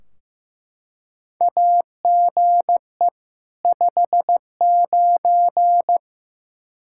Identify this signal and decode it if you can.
Morse code, heard as a keyed tone, about 16 wpm, 700 Hz. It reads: AGE 59
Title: Practice QSO - Exchange 3 - More Info - 15wpm